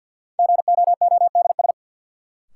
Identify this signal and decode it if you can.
Morse code, heard as a keyed tone, about 52 wpm, 700 Hz. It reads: GOODS
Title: Sentences from Top 1000 Words - Review Part 6 - New Word Drill - 50wpm